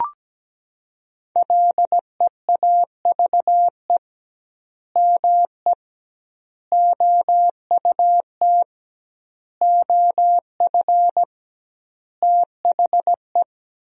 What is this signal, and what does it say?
Morse code, heard as a keyed tone, about 17 wpm, 700 Hz. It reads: LEAVE ME OUT OF THE
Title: 2x Word-Spacing - Sentences from Top 400 Words - 17wpm